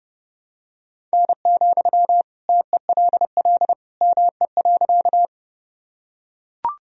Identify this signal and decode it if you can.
Morse code, heard as a keyed tone, about 30 wpm, 700 Hz. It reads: N, TELL ME.
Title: Sentences from Top 200 Words and Abbreviations 30wpm